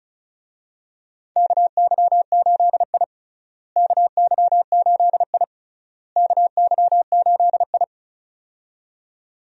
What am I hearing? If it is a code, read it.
Morse code, heard as a keyed tone, about 36 wpm, 700 Hz. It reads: KY8I KY8I KY8I
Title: USA Call Signs - 4 Characters - Repeated 3x - 35wpm